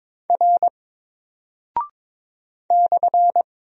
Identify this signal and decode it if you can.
Morse code, heard as a keyed tone, about 22 wpm, 700 Hz. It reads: R /
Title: Single Character (Letter:Number:Punctuation:Prosign) 22wpm